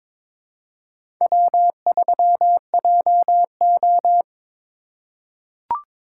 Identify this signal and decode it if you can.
Morse code, heard as a keyed tone, about 22 wpm, 700 Hz. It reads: W3JO
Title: USA Call Signs - 4 Characters - 8x Character Spacing - 22wpm